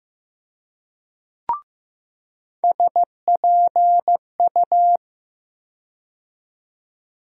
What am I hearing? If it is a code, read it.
Morse code, heard as a keyed tone, about 15 wpm, 700 Hz. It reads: SPU
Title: USA Call Sign Postfixes - 3 Letters 15wpm